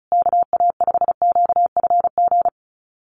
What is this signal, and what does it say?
Morse code, heard as a keyed tone, about 34 wpm, 700 Hz. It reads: KA5QFG
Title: USA Call Signs Repeated 3x 35wpm